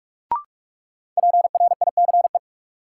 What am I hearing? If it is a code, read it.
Morse code, heard as a keyed tone, about 46 wpm, 700 Hz. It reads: PRICE
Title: Sentences from Top 600 Words - New Word Drill - 45wpm